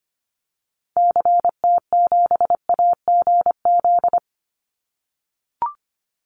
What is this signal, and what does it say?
Morse code, heard as a keyed tone, about 25 wpm, 700 Hz. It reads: CT7AGZ
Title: Call Signs - 25wpm